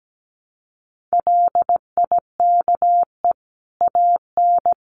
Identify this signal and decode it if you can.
Morse code, heard as a keyed tone, about 17 wpm, 700 Hz. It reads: LIKE AN
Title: A Sound Of Thunder 17wpm